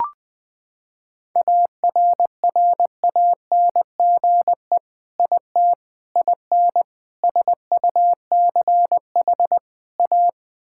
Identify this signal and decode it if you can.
Morse code, heard as a keyed tone, about 20 wpm, 700 Hz. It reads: ARRANGE IT IN SUCH A
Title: Sentences with 5-Word Phrases - 20wpm